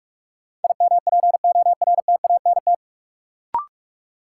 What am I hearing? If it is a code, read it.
Morse code, heard as a keyed tone, about 46 wpm, 700 Hz. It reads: IMPORTANT
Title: Sets of 3 Words - New Word Drill - 45wpm